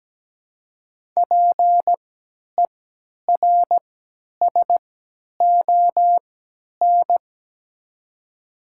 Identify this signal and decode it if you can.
Morse code, heard as a keyed tone, about 17 wpm, spaced as for 10 wpm, 700 Hz. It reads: PERSON
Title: 3x Character Spacing - Top 500 Words - 17wpm